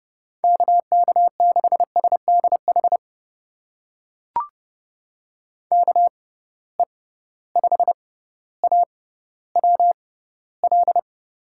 Morse code, heard as a keyed tone, about 30 wpm, spaced as for 15 wpm, 700 Hz. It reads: KK6SDH KE5AWL
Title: USA Call Signs - 6 Characters - 6x Character Spacing - 30wpm